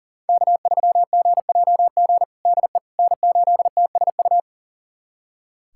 Morse code, heard as a keyed tone, about 40 wpm, 700 Hz. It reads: K3GJG DE N8TSU
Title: USA Callsign DE Callsign - 5 Characters - 40wpm